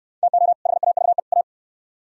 Morse code, heard as a keyed tone, about 64 wpm, 700 Hz. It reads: TO HAVE A